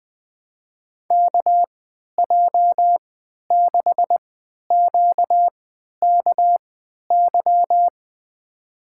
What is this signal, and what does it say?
Morse code, heard as a keyed tone, about 20 wpm, spaced as for 11 wpm, 700 Hz. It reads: KJ6QKY
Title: USA Call Signs - 6 Characters - 3x Character Spacing - 20wpm